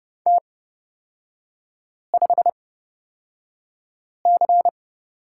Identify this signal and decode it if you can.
Morse code, heard as a keyed tone, about 30 wpm, spaced as for 5 wpm, 700 Hz. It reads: T5C
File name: Lesson 37e - Letter x - Three Characters 30wpm